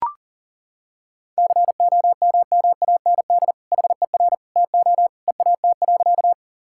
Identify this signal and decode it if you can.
Morse code, heard as a keyed tone, about 37 wpm, 700 Hz. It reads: COMMAND HER TO EAT.
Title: Sentences from Top 1600 Words 40wpm